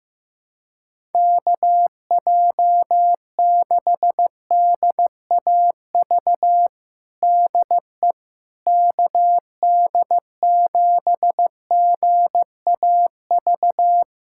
Morse code, heard as a keyed tone, about 15 wpm, 700 Hz. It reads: KJ6DAV DE KD7GAV
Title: USA Callsign DE Callsign - 6 Characters - 15wpm